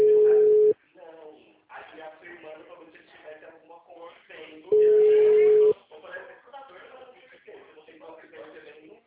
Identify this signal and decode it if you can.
Ringback tone (Europe, 425 Hz), with speech.